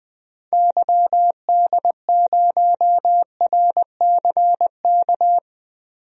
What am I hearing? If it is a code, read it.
Morse code, heard as a keyed tone, about 20 wpm, 700 Hz. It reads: YD0RCK
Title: DXCC-100 Call Signs - 6 Characters - Repeated 3x - 20wpm